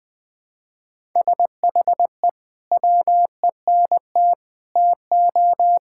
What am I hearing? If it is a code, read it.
Morse code, heard as a keyed tone, about 20 wpm, 700 Hz. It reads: SHE WENT TO
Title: Sentences with 2-Word Phrases - Spoken Beginning - 20wpm